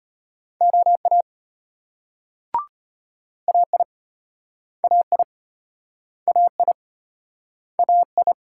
Morse code, heard as a keyed tone, about 35 wpm, 700 Hz. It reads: OA AI AI AI AI